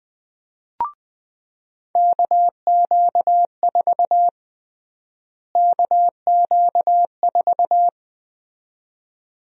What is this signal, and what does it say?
Morse code, heard as a keyed tone, about 20 wpm, 700 Hz. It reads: KQ4 KQ4
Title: USA Call Sign Prefixes - Repeated 2x - 20wpm